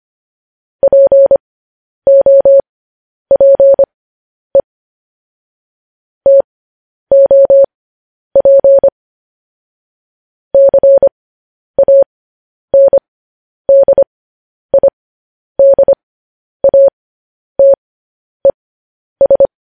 Morse code, heard as a keyed tone, about 25 wpm, spaced as for 10 wpm, 550 Hz. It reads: POPE TOP CANDIDATES
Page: news-headlines-in-morse-code-at-10-wpm